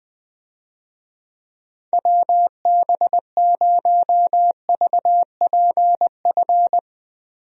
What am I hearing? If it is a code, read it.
Morse code, heard as a keyed tone, about 20 wpm, 700 Hz. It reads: WB0VPF
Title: USA Call Signs - 6 Characters - Repeated 3x - 20wpm